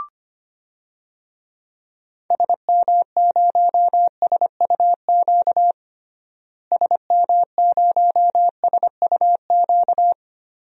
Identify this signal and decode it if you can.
Morse code, heard as a keyed tone, about 25 wpm, 700 Hz. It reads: SM0SUQ SM0SUQ
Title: DXCC-100 Call Signs - 6 Characters - Repeated 3x - 25wpm